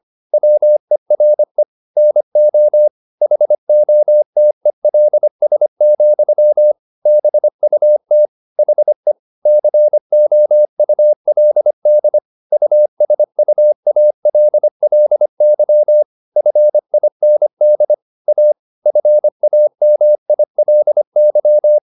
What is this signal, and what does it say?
Morse code, heard as a keyed tone, about 25 wpm, 600 Hz. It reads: WERE NO HOTELS, BUT HE COULD USUALLY FIND A FAMILY